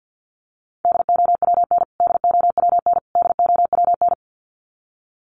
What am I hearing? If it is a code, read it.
Morse code, heard as a keyed tone, about 48 wpm, 700 Hz. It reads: DOWN DOWN DOWN
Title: Top 500 Words Repeated 50wpm